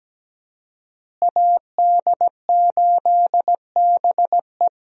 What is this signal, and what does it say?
Morse code, heard as a keyed tone, about 17 wpm, 700 Hz. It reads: AD8BE
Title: USA Call Signs - 5 Characters - Repeated 2x - 17wpm